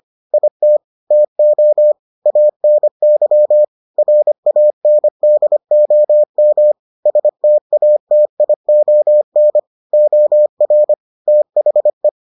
Morse code, heard as a keyed tone, about 25 wpm, 600 Hz. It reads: IT TO ANY RANDOM STATION OR THE